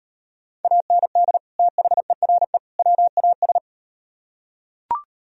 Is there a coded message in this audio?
Yes: Morse code from 0.6 to 3.6 s, heard as a keyed tone, about 38 wpm, 700 Hz. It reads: AND THERE WAS